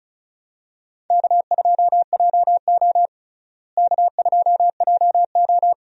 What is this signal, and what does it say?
Morse code, heard as a keyed tone, about 36 wpm, 700 Hz. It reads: K2JO K2JO
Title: USA Call Signs - 4 Characters - Repeated 2x - 35wpm